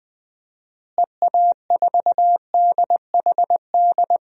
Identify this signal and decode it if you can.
Morse code, heard as a keyed tone, about 20 wpm, 700 Hz. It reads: EA4DHD
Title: DXCC-100 Call Signs - 6 Characters - Repeated 2x - 20wpm